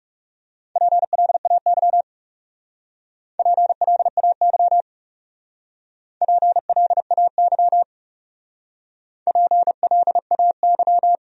Morse code, heard as a keyed tone, about 41 wpm, 700 Hz. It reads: PLAY PLAY PLAY PLAY